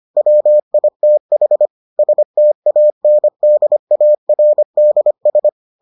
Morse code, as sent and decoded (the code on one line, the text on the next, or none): .-- .. - .... / ... - .- -. -.. .- .-. -.. ...
WITH STANDARDS